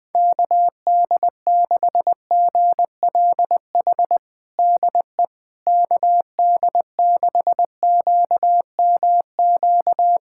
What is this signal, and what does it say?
Morse code, heard as a keyed tone, about 20 wpm, 700 Hz. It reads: KD6GLH DE KD6QMQ
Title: USA Callsign DE Callsign - 6 Characters - 20wpm